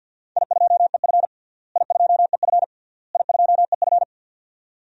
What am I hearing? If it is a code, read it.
Morse code, heard as a keyed tone, about 48 wpm, 700 Hz. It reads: I2EF I2EF I2EF
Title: DXCC-100 Call Signs - 4 Characters - Repeated 3x - 50wpm